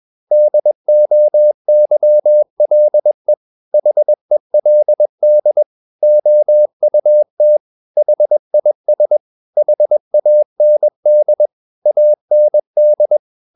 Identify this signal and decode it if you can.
Morse code, heard as a keyed tone, about 21 wpm, 600 Hz. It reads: DOYLE HELD OUT HIS HAND AND
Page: CW med Gnister